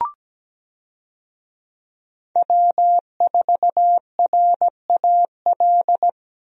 Morse code, heard as a keyed tone, about 17 wpm, 700 Hz. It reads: W4RAL
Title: USA Call Signs - 5 Characters - Repeated 2x - 17wpm